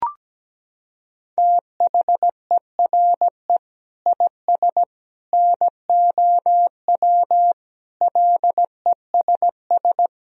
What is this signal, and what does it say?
Morse code, heard as a keyed tone, about 17 wpm, 700 Hz. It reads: THERE IS NOW LESS
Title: Sentences from Top 1000 Words - Review Part 4 - 17wpm